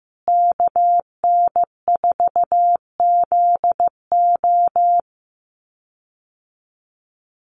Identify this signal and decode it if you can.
Morse code, heard as a keyed tone, about 15 wpm, 700 Hz. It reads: KN4ZO
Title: USA Call Signs Repeated 2x 15wpm